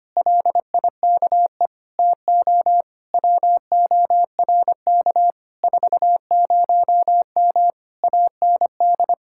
Morse code, heard as a keyed tone, about 26 wpm, 700 Hz. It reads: LIKE TO WORK 40M AND
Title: Speed-Racing Common QSO Phrases - 25wpm